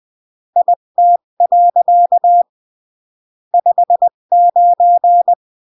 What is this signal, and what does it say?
Morse code, heard as a keyed tone, about 20 wpm, 700 Hz. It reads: IT. 59
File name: CW / jodorowsky / 82 commandments 20-15